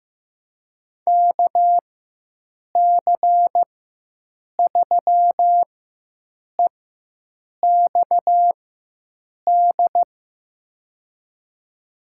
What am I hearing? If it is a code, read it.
Morse code, heard as a keyed tone, about 15 wpm, spaced as for 7 wpm, 700 Hz. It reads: KC3EXD
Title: USA Call Signs - 4x Character Spacing - 15wpm